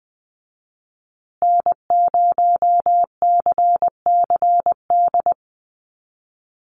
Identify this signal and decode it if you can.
Morse code, heard as a keyed tone, about 20 wpm, 700 Hz. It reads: N0CCD
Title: USA Call Signs Repeated 3x 20wpm